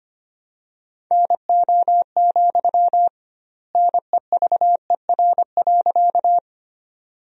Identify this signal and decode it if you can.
Morse code, heard as a keyed tone, about 25 wpm, 700 Hz. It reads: NO, NEVER.
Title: 2x Word-Spacing - Sentences from Top 300 Words - 25wpm